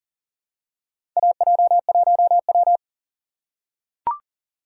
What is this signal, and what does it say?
Morse code, heard as a keyed tone, about 37 wpm, 700 Hz. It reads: AJ1W
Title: USA Call Signs - 4 Characters - 3x Character Spacing - 40wpm